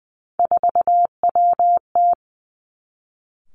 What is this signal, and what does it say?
Morse code, heard as a keyed tone, about 20 wpm, 700 Hz. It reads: 4WT
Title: Three Challenging Letter-Number Combinations 20wpm